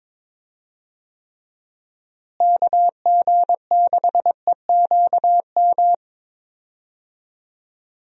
Morse code, heard as a keyed tone, about 22 wpm, 700 Hz. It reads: KG6EQM
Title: USA Call Signs - 6 Characters - Repeated 3x - 22wpm